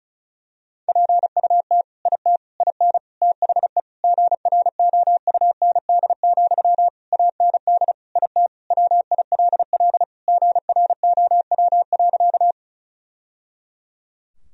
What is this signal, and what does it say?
Morse code, heard as a keyed tone, about 36 wpm, 700 Hz. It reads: PUT IT IN THE GROUND, AND IT WILL GROW.
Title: Sentences from Top 1000 Words - Review Part 2 - 35wpm